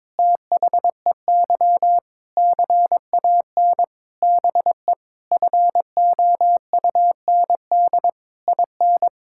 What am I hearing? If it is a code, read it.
Morse code, heard as a keyed tone, about 22 wpm, 700 Hz. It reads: THEY CAN BE FOUND IN
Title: Sentences from Top 1000 Words - Review Part 6 - 22wpm